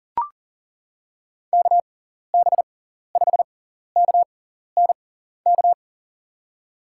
Morse code, heard as a keyed tone, about 40 wpm, spaced as for 14 wpm, 700 Hz. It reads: KB5KNK
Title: USA Call Signs - 6 Characters - 6x Character Spacing - 40wpm